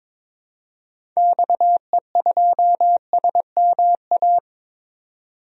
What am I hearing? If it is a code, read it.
Morse code, heard as a keyed tone, about 22 wpm, 700 Hz. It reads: XE2SMA
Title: DXCC-100 Call Signs - 6 Characters - 22wpm